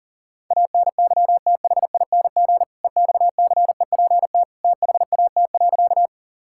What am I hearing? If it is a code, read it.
Morse code, heard as a keyed tone, about 40 wpm, 700 Hz. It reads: ANYTHING EXCEPT THAT.
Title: Sentences from Top 1600 Words 40wpm